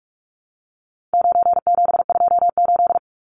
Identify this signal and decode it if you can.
Morse code, heard as a keyed tone, about 44 wpm, 700 Hz. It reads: 9728
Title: Numbers - 4-Digit 45wpm